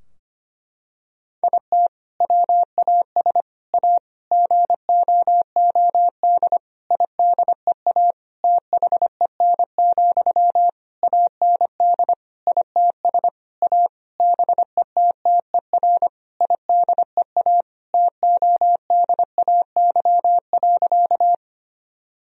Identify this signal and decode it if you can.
Morse code, heard as a keyed tone, about 25 wpm, 700 Hz. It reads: IT WAS A GOOD IDEA THEN, AND ITS A BETTER IDEA TODAY.
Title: Sentences from Top 1000 Words - Review Part 4 - 25wpm